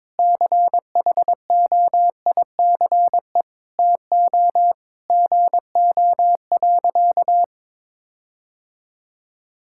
Morse code, heard as a keyed tone, about 22 wpm, 700 Hz. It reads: CHOICE TO GO.
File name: practice / 22wpm / Sentences with 4-Word Phrases - 22wpm